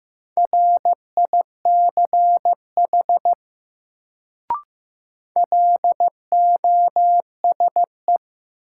Morse code, heard as a keyed tone, about 16 wpm, 700 Hz. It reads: RICH LOSE
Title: Sentences from Top 1100 Words - New Word Drill - 15wpm